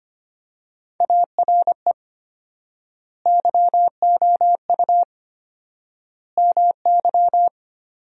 Morse code, heard as a keyed tone, about 25 wpm, 700 Hz. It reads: ARE YOU MY
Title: 4x Word-Spacing - Sentences from Top 300 Words - 25wpm